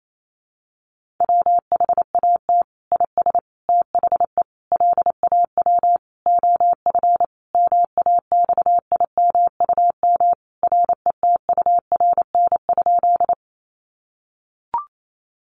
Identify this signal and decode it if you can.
Morse code, heard as a keyed tone, about 28 wpm, 700 Hz. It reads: WHAT IS THE LAW OF MAXIMUM RETURN?
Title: Sentences from Top 1300 Words 28wpm